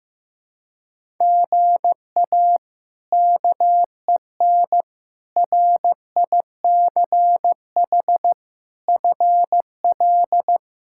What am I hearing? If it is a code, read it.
Morse code, heard as a keyed tone, about 15 wpm, 700 Hz. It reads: GA KEN RICH FL
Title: K1USN SST QSO Practice - 15wpm